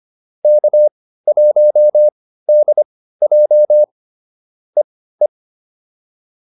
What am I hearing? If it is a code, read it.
Morse code, heard as a keyed tone, about 25 wpm, spaced as for 15 wpm, 600 Hz. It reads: K1DJ EE